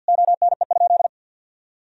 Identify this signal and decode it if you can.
Morse code, heard as a keyed tone, about 50 wpm, 700 Hz. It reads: ONE?
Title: Sentences from Top 1000 Words - Review Part 4 - 50wpm